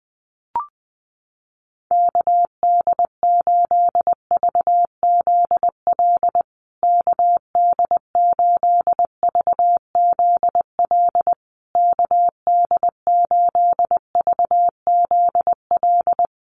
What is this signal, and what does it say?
Morse code, heard as a keyed tone, about 20 wpm, 700 Hz. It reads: KD8VZL KD8VZL KD8VZL
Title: USA Call Signs Repeated 3x 20wpm